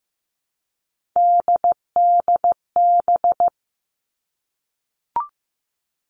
Morse code, heard as a keyed tone, about 15 wpm, 700 Hz. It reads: DDB
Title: All Cut Numbers - 3-Digit 15wpm